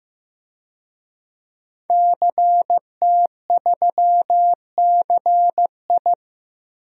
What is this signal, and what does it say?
Morse code, heard as a keyed tone, about 15 wpm, 700 Hz. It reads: CT3CI
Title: DXCC-100 Call Signs - 5 Characters - Repeated 3x - 15wpm